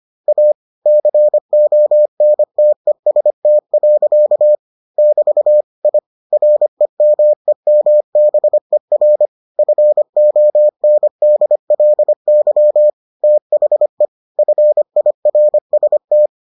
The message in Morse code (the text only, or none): A CONTEST. <BT> I REMEMBER FONDLY THE FIRST